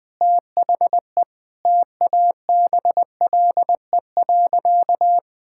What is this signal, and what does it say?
Morse code, heard as a keyed tone, about 20 wpm, 700 Hz. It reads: THE TABLE.
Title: Sentences with 3-Word Phrases - 20wpm